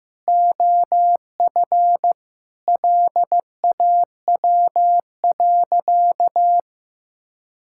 Morse code, heard as a keyed tone, about 15 wpm, 700 Hz. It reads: OF LAW.
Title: Sentences from Top 1000 Words - Review Part 5 - 15wpm